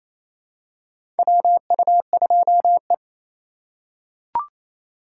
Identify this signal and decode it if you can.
Morse code, heard as a keyed tone, about 28 wpm, 700 Hz. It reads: WU2E
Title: USA Call Signs - 4 Characters - 3x Character Spacing - 28wpm